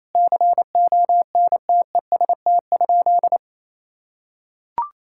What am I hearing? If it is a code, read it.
Morse code, heard as a keyed tone, about 28 wpm, 700 Hz. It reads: CONTEST?
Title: Sentences from Top 1600 Words 28wpm